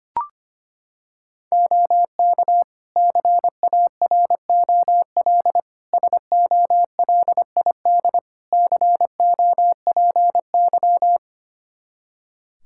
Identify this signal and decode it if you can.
Morse code, heard as a keyed tone, about 25 wpm, 700 Hz. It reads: OK CAROL SOLID COPY
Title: K1USN SST QSO Practice 25wpm